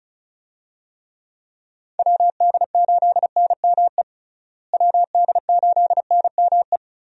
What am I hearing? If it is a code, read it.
Morse code, heard as a keyed tone, about 34 wpm, 700 Hz. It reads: WD8NME WD8NME
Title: USA Call Signs - 6 Characters - Repeated 2x - 35wpm